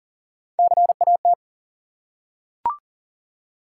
Morse code, heard as a keyed tone, about 40 wpm, 700 Hz. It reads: CAT
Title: Three Letter Words - 40wpm